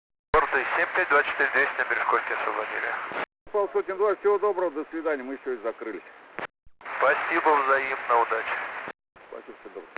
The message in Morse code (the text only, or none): none